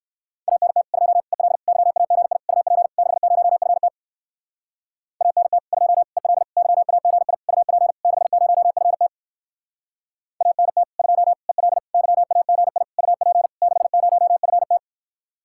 Morse code, heard as a keyed tone, about 58 wpm, 700 Hz. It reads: ANT 2 EL YAGI UP 60FT ANT 2 EL YAGI UP 60FT ANT 2 EL YAGI UP 60FT